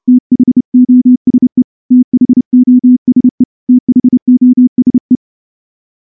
Morse code, heard as a keyed tone, about 28 wpm, 250 Hz. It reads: THOSE THOSE THOSE
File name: morse-training-30WPM